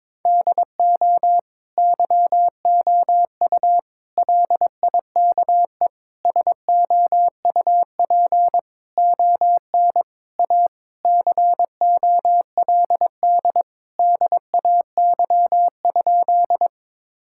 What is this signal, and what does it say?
Morse code, heard as a keyed tone, about 22 wpm, 700 Hz. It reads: DO YOU LIKE SOUP ON A COLD DAY?
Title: Sentences from Top 1000 Words - Review Part 6 - 22wpm